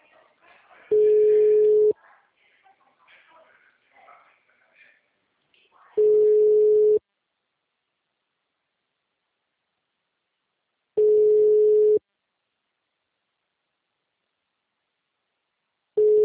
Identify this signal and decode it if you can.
Ringback tone (Europe, 425 Hz), with speech.
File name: out- -2058- - - .32866